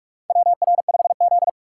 Morse code, heard as a keyed tone, about 46 wpm, 700 Hz. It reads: WR5Z
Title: USA Call Signs - 4 Characters - 4x Character Spacing - 45wpm